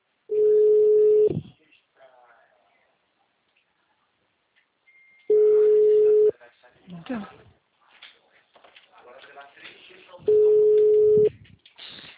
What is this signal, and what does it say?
Ringback tone (Europe, 425 Hz), with speech.